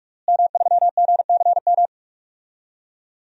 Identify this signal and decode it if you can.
Morse code, heard as a keyed tone, about 41 wpm, 700 Hz. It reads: M3GCM
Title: DXCC-100 Call Signs - 5 Characters - Repeated 2x - 45wpm